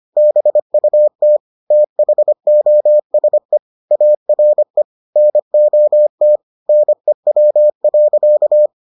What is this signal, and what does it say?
Morse code, heard as a keyed tone, about 25 wpm, 600 Hz. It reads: BUT THOSE ARE NOT NEW.